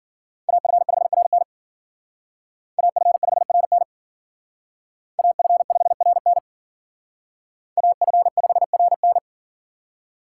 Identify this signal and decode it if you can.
Morse code, heard as a keyed tone, about 51 wpm, 700 Hz. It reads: AF5RN AF5RN AF5RN AF5RN